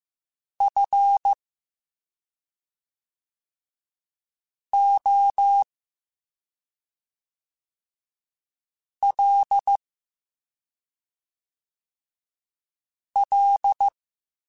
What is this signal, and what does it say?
Morse code, heard as a keyed tone, about 15 wpm, 800 Hz. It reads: F O L L